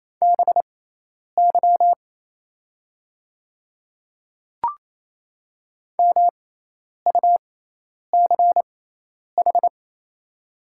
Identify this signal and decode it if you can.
Morse code, heard as a keyed tone, about 28 wpm, spaced as for 10 wpm, 690 Hz. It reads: BY MUCH
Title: 6x Character Spacing - Top 200 Words - 28wpm